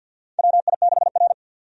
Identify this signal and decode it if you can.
Morse code, heard as a keyed tone, about 53 wpm, 700 Hz. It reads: WI6R
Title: USA Call Signs - 4 Characters - 50wpm